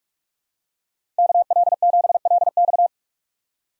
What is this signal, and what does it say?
Morse code, heard as a keyed tone, about 42 wpm, 700 Hz. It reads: KL7LX
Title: DXCC-100 Call Signs - 5 Characters - 45wpm